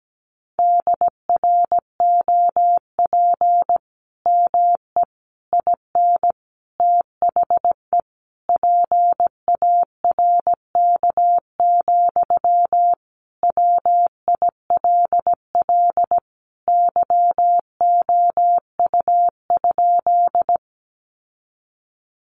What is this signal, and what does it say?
Morse code, heard as a keyed tone, about 17 wpm, 700 Hz. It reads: DROP ME IN THE PARK, WILL YOU?
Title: Sentences from Top 1400 Words 17wpm